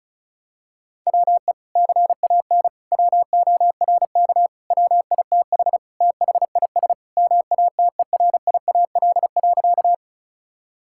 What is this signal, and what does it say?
Morse code, heard as a keyed tone, about 34 wpm, 700 Hz. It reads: WE CAN WORK WITH THIS MATERIAL.
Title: Sentences from Top 1000 Words - Review Part 5 - 35wpm